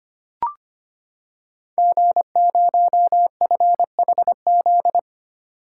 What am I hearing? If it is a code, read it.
Morse code, heard as a keyed tone, about 25 wpm, 700 Hz. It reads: G0FHZ
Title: Code Groups - 5 Letter:Numbers - 25wpm